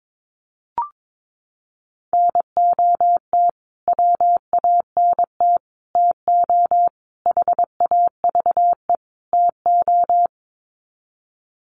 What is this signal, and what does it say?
Morse code, heard as a keyed tone, about 22 wpm, 700 Hz. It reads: NOT WANT TO HAVE TO
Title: Sets of 5 Words Encore2 22wpm